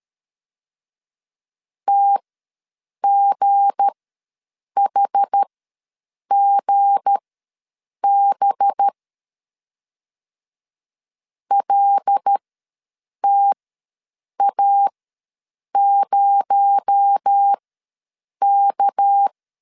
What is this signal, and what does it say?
Morse code, heard as a keyed tone, about 13 wpm, spaced as for 7 wpm, 800 Hz. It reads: TGHGB LTA0K
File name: LEZIONI / 7 GF9Z / 7 G ws 12wpm 3140